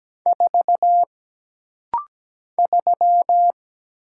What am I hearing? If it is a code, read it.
Morse code, heard as a keyed tone, about 17 wpm, 700 Hz. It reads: 4 3
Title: Single Letter-Number - Rapid-Fire 17wpm